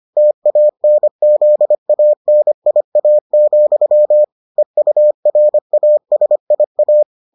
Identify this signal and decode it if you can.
Morse code, heard as a keyed tone, about 25 wpm, 600 Hz. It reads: TANZANIA, EURASIA